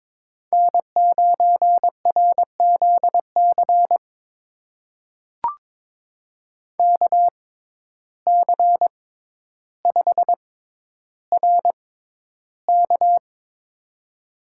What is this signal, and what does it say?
Morse code, heard as a keyed tone, about 22 wpm, spaced as for 11 wpm, 700 Hz. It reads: N9RZC KC5RK
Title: USA Call Signs - 5 Characters - 6x Character Spacing - 22wpm